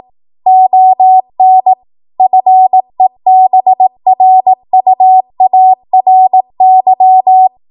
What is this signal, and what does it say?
Morse code, heard as a keyed tone, about 18 wpm, 750 Hz. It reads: ON FEBRUARY